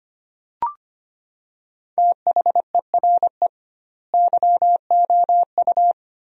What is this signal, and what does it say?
Morse code, heard as a keyed tone, about 25 wpm, 700 Hz. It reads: THERE YOU
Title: 2x Word-Spacing - Sentences from Top 100 Words - 25wpm